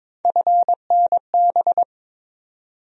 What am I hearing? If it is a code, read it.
Morse code, heard as a keyed tone, about 22 wpm, 700 Hz. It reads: FNB
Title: USA Call Sign Postfixes - 3 Letters 22wpm